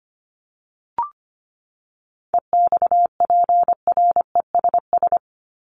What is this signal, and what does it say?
Morse code, heard as a keyed tone, about 25 wpm, 700 Hz. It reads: EXPRESS
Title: ED Suffix - Top 100 Words with Base Words 25wpm